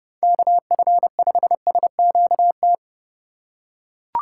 Morse code, heard as a keyed tone, about 30 wpm, 700 Hz. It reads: KF5SQT
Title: USA Call Signs - 3x Character Spacing - 30wpm